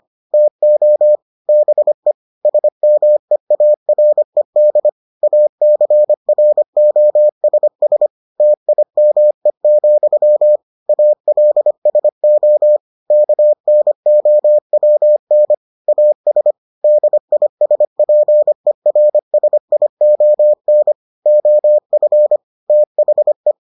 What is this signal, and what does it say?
Morse code, heard as a keyed tone, about 25 wpm, 600 Hz. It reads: TO BE SMEARED ACROSS TIME, ALSO KNOWN AS DISPERSION OF THE